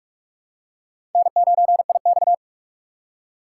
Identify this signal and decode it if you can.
Morse code, heard as a keyed tone, about 46 wpm, 700 Hz. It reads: N9IX
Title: USA Call Signs - 4x Character Spacing - 45wpm